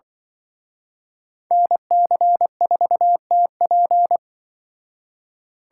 Morse code, heard as a keyed tone, about 24 wpm, 700 Hz. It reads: NC4TP
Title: Speed Racing - USA 2x2 Call Signs 20wpm